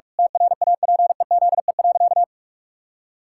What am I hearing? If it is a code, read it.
Morse code, heard as a keyed tone, about 41 wpm, 700 Hz. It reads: TRAPEZE.
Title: Pangram Challenge v2 - 45wpm